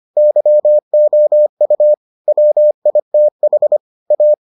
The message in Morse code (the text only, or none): YOU WITH A